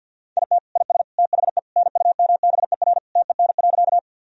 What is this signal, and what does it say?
Morse code, heard as a keyed tone, about 50 wpm, 700 Hz. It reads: IT IS THE NUMBER TEN.
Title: Sentences from Top 1000 Words - Review Part 2 - 50wpm